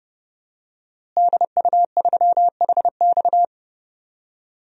Morse code, heard as a keyed tone, about 32 wpm, 700 Hz. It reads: DU3HX
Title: DXCC-100 Call Signs - 5 Characters - Repeated 2x - 30wpm